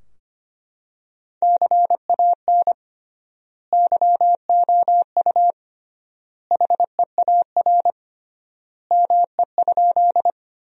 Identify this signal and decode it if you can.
Morse code, heard as a keyed tone, about 25 wpm, 700 Hz. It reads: CAN YOU HEAR ME?
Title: 3x Word-Spacing - Sentences from Top 300 Words - 25wpm